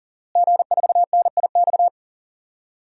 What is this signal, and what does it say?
Morse code, heard as a keyed tone, about 37 wpm, 700 Hz. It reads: G4NIX
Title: DXCC-100 Call Signs - 5 Characters - Repeated 3x - 40wpm